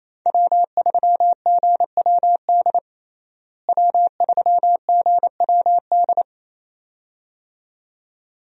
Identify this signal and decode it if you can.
Morse code, heard as a keyed tone, about 28 wpm, 700 Hz. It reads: W3GWD W3GWD
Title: USA Call Signs - 5 Characters - Repeated 2x - 28wpm